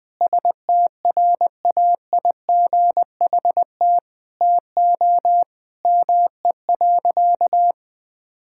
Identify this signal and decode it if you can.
Morse code, heard as a keyed tone, about 20 wpm, 700 Hz. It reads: STRAIGHT TO ME.
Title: Sentences from Top 1000 Words - Review Part 6 - 20wpm